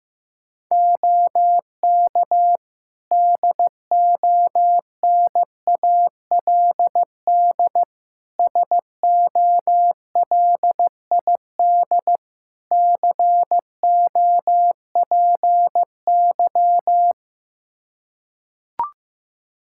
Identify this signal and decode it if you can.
Morse code, heard as a keyed tone, about 15 wpm, 700 Hz. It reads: OK DONALD SOLID COPY
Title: Practice QSO - Exchange 3 - More Info - 15wpm